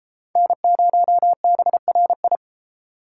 Morse code, heard as a keyed tone, about 30 wpm, 700 Hz. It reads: N0BRI